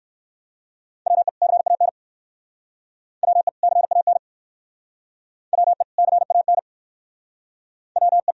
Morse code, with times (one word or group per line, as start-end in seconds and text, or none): 1.1-1.3 WE
1.4-1.9 CAN
3.2-3.5 WE
3.6-4.2 CAN
5.5-5.8 WE
6.0-6.6 CAN
8.0-8.3 WE